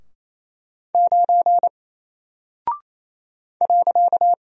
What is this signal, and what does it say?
Morse code, heard as a keyed tone, about 28 wpm, 700 Hz. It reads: 9 .
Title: Single Character (Letter:Number:Punctuation:Prosign) Alt 28wpm